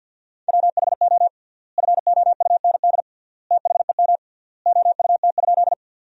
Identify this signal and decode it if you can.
Morse code, heard as a keyed tone, about 52 wpm, 700 Hz. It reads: WHO FOUND THEM OUT?
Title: 3x Word-Spacing - Sentences from Top 300 Words - 50wpm